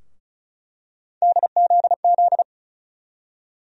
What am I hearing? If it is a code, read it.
Morse code, heard as a keyed tone, about 34 wpm, 700 Hz. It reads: DZZ
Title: USA Call Sign Postfixes - 3 Letters 35wpm